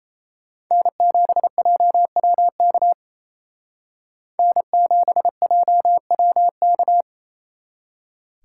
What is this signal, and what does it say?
Morse code, heard as a keyed tone, about 36 wpm, 700 Hz. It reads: N7JWK N7JWK